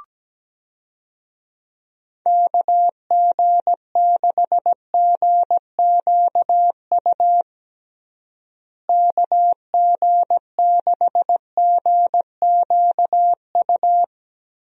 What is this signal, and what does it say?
Morse code, heard as a keyed tone, about 17 wpm, 700 Hz. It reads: KG6GQU KG6GQU
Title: USA Call Signs - 6 Characters - Repeated 3x - 17wpm